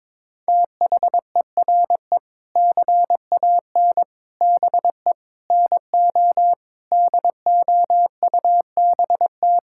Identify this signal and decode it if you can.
Morse code, heard as a keyed tone, about 21 wpm, 700 Hz. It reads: THERE CAN BE NO DOUBT
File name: practice / 22wpm / Sets of 7 Words - 22wpm